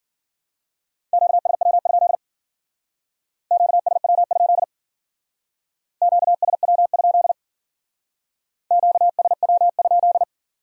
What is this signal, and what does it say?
Morse code, heard as a keyed tone, about 49 wpm, 700 Hz. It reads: QSW? QSW? QSW? QSW?
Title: Speed-Racing Q-Codes - 40wpm